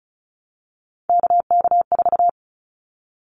Morse code, heard as a keyed tone, about 34 wpm, 700 Hz. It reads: KK4
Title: Callsign Prefixes for Top 25 DXCC - Repeated 3x - 35wpm